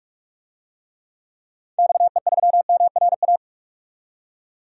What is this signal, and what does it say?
Morse code, heard as a keyed tone, about 46 wpm, 700 Hz. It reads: XE3MRA
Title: DXCC-100 Call Signs - 6 Characters - 45wpm